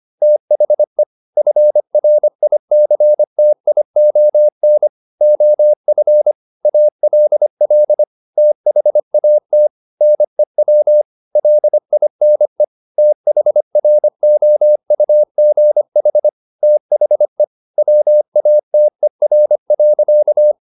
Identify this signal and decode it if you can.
Morse code, heard as a keyed tone, about 25 wpm, 600 Hz. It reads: THE FRICTION OF ALL THAT NEW LINE THROUGH THE WATER.